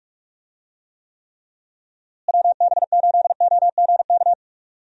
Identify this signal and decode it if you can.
Morse code, heard as a keyed tone, about 46 wpm, 700 Hz. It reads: WB8OGK
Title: USA Call Signs - 6 Characters - Repeated 3x - 45wpm